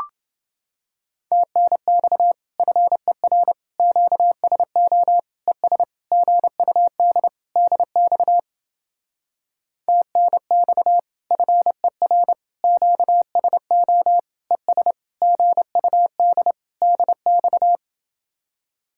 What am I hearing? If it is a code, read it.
Morse code, heard as a keyed tone, about 29 wpm, 700 Hz. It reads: TNX FER QSO ES GUD DX TNX FER QSO ES GUD DX